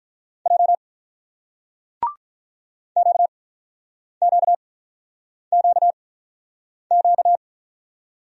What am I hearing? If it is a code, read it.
Morse code, heard as a keyed tone, about 44 wpm, 700 Hz. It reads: J Q Q Q Q
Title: Speed-Racing Single Letters 35wpm